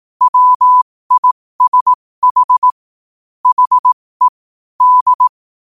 Morse code, heard as a keyed tone, about 18 wpm, spaced as for 15 wpm, 1000 Hz. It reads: WISH HE D
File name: qotdfiles / 15wpm